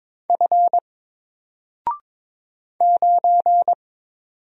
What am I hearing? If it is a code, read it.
Morse code, heard as a keyed tone, about 22 wpm, 700 Hz. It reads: F 9
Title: Single Character (Letter:Number:Punctuation:Prosign) 22wpm